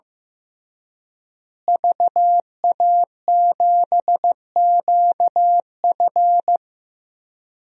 Morse code, heard as a keyed tone, about 15 wpm, 700 Hz. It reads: VA7QF
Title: DXCC-100 Call Signs - 5 Characters - Repeated 3x - 15wpm